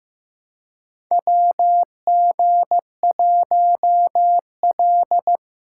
Morse code, heard as a keyed tone, about 15 wpm, 700 Hz. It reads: WG1L
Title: USA Call Signs - 4 Characters - 3x Character Spacing - 15wpm